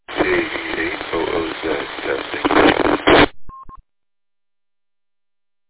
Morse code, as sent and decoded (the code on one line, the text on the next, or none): none